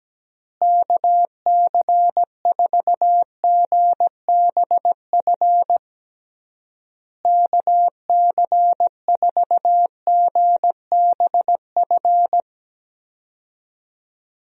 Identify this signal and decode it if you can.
Morse code, heard as a keyed tone, about 17 wpm, 700 Hz. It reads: KC4GBF KC4GBF
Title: USA Call Signs - 6 Characters - Repeated 3x - 17wpm